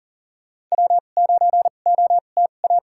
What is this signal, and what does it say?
Morse code, heard as a keyed tone, about 40 wpm, spaced as for 29 wpm, 700 Hz. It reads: W9OTA
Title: USA Call Signs - 2x Character Spacing - 40wpm